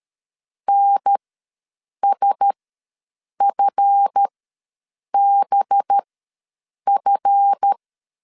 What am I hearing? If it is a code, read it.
Morse code, heard as a keyed tone, about 13 wpm, spaced as for 7 wpm, 800 Hz. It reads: NSFBF
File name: LEZIONI / 9 2 ws 12wpm 3406